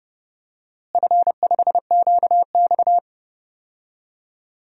Morse code, heard as a keyed tone, about 32 wpm, 700 Hz. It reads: F5QX
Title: DXCC-100 Call Signs - 4 Characters - Repeated 2x - 30wpm